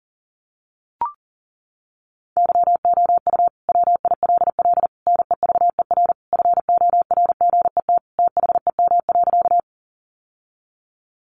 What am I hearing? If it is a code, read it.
Morse code, heard as a keyed tone, about 40 wpm, 700 Hz. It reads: YOU WILL NEVER FORGET THEM.
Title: Sentences from Top 1200 Words 40wpm